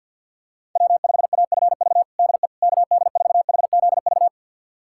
Morse code, heard as a keyed tone, about 50 wpm, 700 Hz. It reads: W5AFV DE KD4HZV
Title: USA Callsign DE Callsign 50wpm